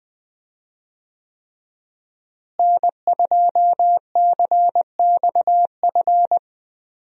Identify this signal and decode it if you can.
Morse code, heard as a keyed tone, about 20 wpm, 700 Hz. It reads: N2CXF
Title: USA Call Signs - 5 Characters - Repeated 3x - 20wpm